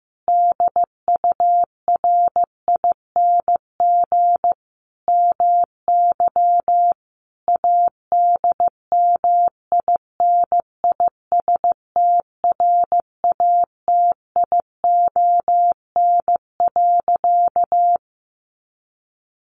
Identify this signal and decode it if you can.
Morse code, heard as a keyed tone, about 15 wpm, 700 Hz. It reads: DURING MY ADMINISTRATION.
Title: Sentences from Top 1300 Words 15wpm